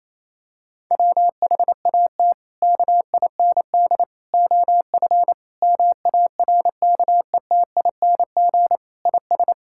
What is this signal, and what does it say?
Morse code, heard as a keyed tone, about 28 wpm, 700 Hz. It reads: WHAT KIND OF MARKETING IS
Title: Sentences from Top 1000 Words - Review Part 5 - 28wpm